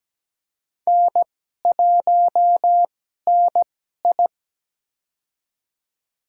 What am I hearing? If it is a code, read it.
Morse code, heard as a keyed tone, about 17 wpm, spaced as for 12 wpm, 700 Hz. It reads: N1NI
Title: USA Call Signs - 4 Characters - 2x Character Spacing - 17wpm